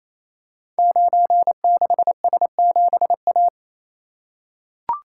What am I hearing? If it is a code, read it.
Morse code, heard as a keyed tone, about 27 wpm, 700 Hz. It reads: 96S7A
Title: Code Groups - 5 Letter:Numbers - 28wpm